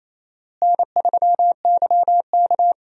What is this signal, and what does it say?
Morse code, heard as a keyed tone, about 28 wpm, 700 Hz. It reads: N3YK
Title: USA Call Signs - 4 Characters - Repeated 2x - 28wpm